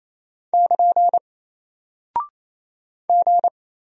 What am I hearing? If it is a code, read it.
Morse code, heard as a keyed tone, about 27 wpm, 700 Hz. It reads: <KN> G
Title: Single Character (Letter:Number:Punctuation:Prosign) Alt 28wpm